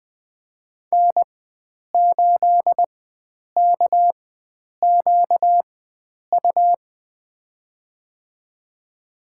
Morse code, heard as a keyed tone, about 20 wpm, spaced as for 9 wpm, 700 Hz. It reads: N8KQU
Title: USA Call Signs - 5 Characters - 4x Character Spacing - 20wpm